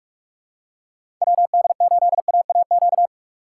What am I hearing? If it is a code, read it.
Morse code, heard as a keyed tone, about 44 wpm, 700 Hz. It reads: WD8AAQ
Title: USA Call Signs - 6 Characters - 6x Character Spacing - 45wpm